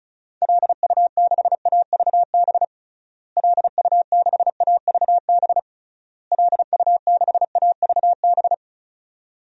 Morse code, heard as a keyed tone, about 36 wpm, 700 Hz. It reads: LU6AVB LU6AVB LU6AVB
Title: DXCC-100 Call Signs - 6 Characters - Repeated 3x - 35wpm